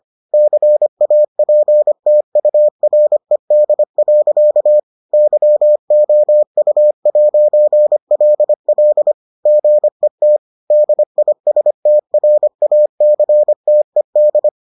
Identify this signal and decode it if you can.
Morse code, heard as a keyed tone, about 25 wpm, 600 Hz. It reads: CAPTURED. YOU'LL GET DISTRACTED